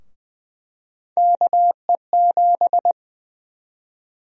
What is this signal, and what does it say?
Morse code, heard as a keyed tone, about 20 wpm, 700 Hz. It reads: KE7
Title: USA Call Sign Prefixes - 20wpm